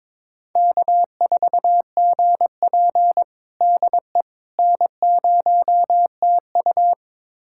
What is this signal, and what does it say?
Morse code, heard as a keyed tone, about 22 wpm, 700 Hz. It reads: K4GP DE N0TU
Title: USA Callsign DE Callsign - 4 Characters - 22wpm